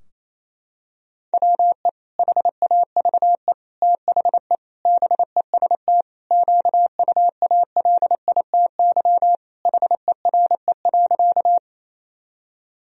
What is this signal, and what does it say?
Morse code, heard as a keyed tone, about 28 wpm, 700 Hz. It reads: WE HAVE THE BEST QUALITY HERE.
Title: Sentences from Top 1000 Words - Review Part 4 - 28wpm